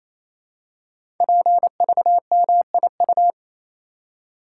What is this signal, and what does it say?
Morse code, heard as a keyed tone, about 28 wpm, 700 Hz. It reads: PVMIU
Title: Code Groups - 5 Letter:Numbers - 28wpm